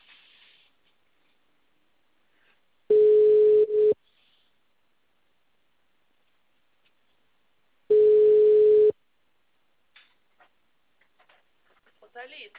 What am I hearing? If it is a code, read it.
Ringback tone (Europe, 425 Hz), with speech.
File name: out- -2046- - - .24869